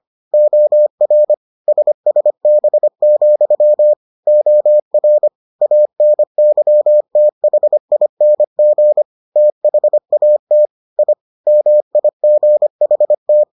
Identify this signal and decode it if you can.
Morse code, heard as a keyed tone, about 25 wpm, 600 Hz. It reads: OR SSB, OR ANYTHING THAT I MIGHT